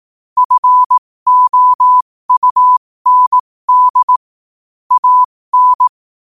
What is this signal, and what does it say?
Morse code, heard as a keyed tone, about 18 wpm, spaced as for 15 wpm, 1000 Hz. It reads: FOUND AN
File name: qotdfiles / 15wpm